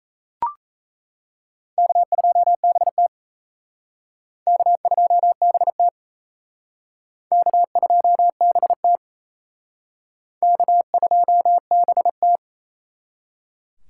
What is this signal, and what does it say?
Morse code, heard as a keyed tone, about 34 wpm, 700 Hz. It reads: K2BT K2BT K2BT K2BT
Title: Speed Racing - USA 1x2 Call Signs 28wpm